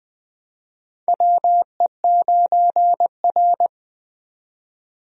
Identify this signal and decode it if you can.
Morse code, heard as a keyed tone, about 20 wpm, 700 Hz. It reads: WE9R
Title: USA Call Signs - 4 Characters - 20wpm